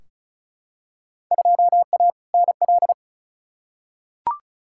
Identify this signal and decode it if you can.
Morse code, heard as a keyed tone, about 36 wpm, 700 Hz. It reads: 2A NL
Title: Field Day Exchanges Encore - 35wpm